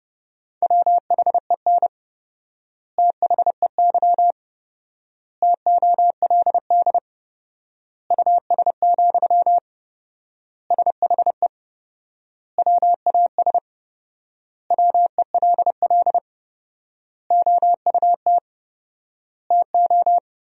Morse code, heard as a keyed tone, about 30 wpm, 700 Hz. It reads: WHEN THEY TOLD US, SHE WAS WELL OUT TO
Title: 4x Word-Spacing - Sentences from Top 400 Words - 30wpm